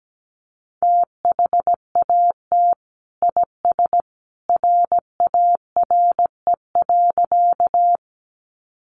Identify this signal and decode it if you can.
Morse code, heard as a keyed tone, about 17 wpm, 700 Hz. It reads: THAT IS RARE.
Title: Sentences from Top 1100 Words 17wpm